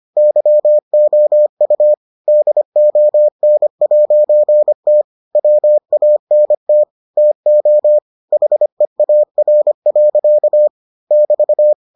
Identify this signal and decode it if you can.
Morse code, heard as a keyed tone, about 25 wpm, 600 Hz. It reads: YOU DON'T WANT TO HEAR. <BT>